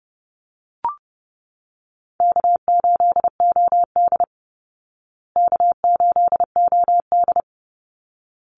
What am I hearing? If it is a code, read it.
Morse code, heard as a keyed tone, about 30 wpm, 700 Hz. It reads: K8OD K8OD
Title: Call Signs Repeated 3x - 30wpm